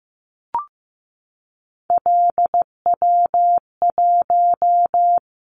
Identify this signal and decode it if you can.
Morse code, heard as a keyed tone, about 15 wpm, 700 Hz. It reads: LW1
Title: Callsign Prefixes for Top 50 DXCC - 15wpm